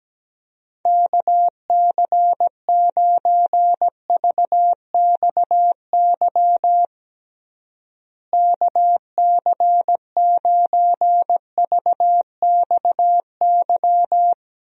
Morse code, heard as a keyed tone, about 17 wpm, 700 Hz. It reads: KC9VXY KC9VXY
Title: USA Call Signs - 6 Characters - Repeated 2x - 17wpm